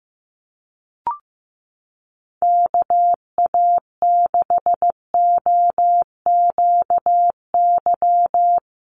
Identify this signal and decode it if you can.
Morse code, heard as a keyed tone, about 15 wpm, 700 Hz. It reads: KA6OQY
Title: USA Call Signs Repeated 2x 15wpm